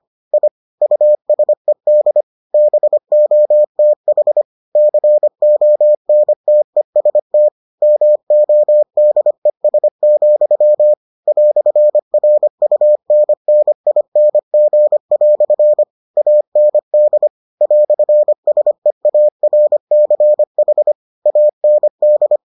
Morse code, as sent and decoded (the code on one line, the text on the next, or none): .. / ..- ... -.. / -... --- - .... / -.-. --- -. - . ... - / -- --- -.. ... --..-- / .-..-. .-. ..- -. -. .. -. --. .-..-. / .- -. -.. / .-..-. ... .- .-. -.-. .... / .- -. -..
I USED BOTH CONTEST MODES, "RUNNING" AND "SEARCH AND